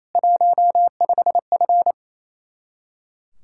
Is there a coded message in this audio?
Yes: Morse code from 0.2 to 1.9 s, heard as a keyed tone, about 35 wpm, 700 Hz. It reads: 15F